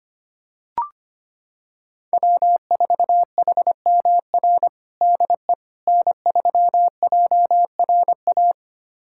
Morse code, heard as a keyed tone, about 25 wpm, 700 Hz. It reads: W4HMR DE N3JRA
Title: USA Callsign DE Callsign - 5 Characters - 25wpm